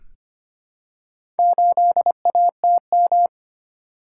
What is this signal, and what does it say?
Morse code, heard as a keyed tone, about 25 wpm, 700 Hz. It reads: 8ATM
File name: practice / 25wpm / Four Letter-Number Combinations - 25wpm